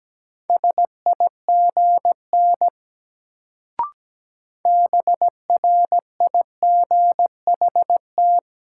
Morse code, heard as a keyed tone, about 17 wpm, 700 Hz. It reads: SIGN BRIGHT
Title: Sentences from Top 1000 Words - New Word Drill - 17wpm